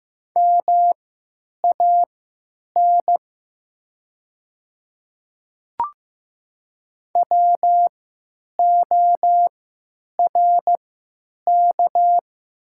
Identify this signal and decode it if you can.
Morse code, heard as a keyed tone, about 15 wpm, spaced as for 8 wpm, 700 Hz. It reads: MAN WORK
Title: 3x Character Spacing - Top 200 Words - 15wpm